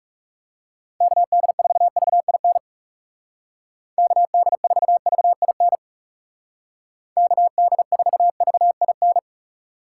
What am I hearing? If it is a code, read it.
Morse code, heard as a keyed tone, about 40 wpm, 700 Hz. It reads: KD4VIN KD4VIN KD4VIN